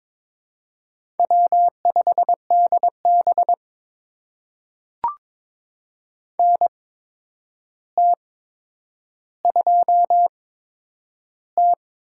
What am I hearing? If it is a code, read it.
Morse code, heard as a keyed tone, about 22 wpm, spaced as for 9 wpm, 700 Hz. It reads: W5DB NT2T
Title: USA Call Signs - 4 Characters - 8x Character Spacing - 22wpm